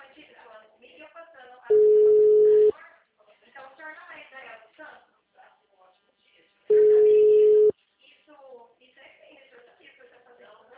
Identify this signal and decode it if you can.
Ringback tone (Europe, 425 Hz), with speech.